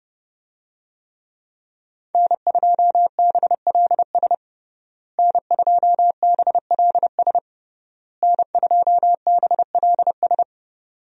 Morse code, heard as a keyed tone, about 30 wpm, 700 Hz. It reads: N2BLS N2BLS N2BLS
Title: USA Call Signs - 5 Characters - Repeated 3x - 30wpm